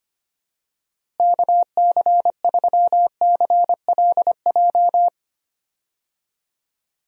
Morse code, heard as a keyed tone, about 25 wpm, 700 Hz. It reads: KC3CLJ
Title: USA Call Signs - 6 Characters - Repeated 3x - 25wpm